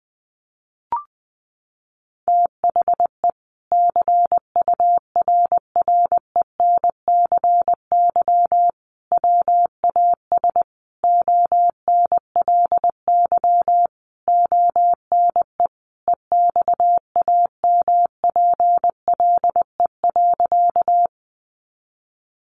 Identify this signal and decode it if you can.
Morse code, heard as a keyed tone, about 20 wpm, 700 Hz. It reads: THE CURRENCY WAS ONLY ONE EXAMPLE.
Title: Sentences from Top 1500 Words 20wpm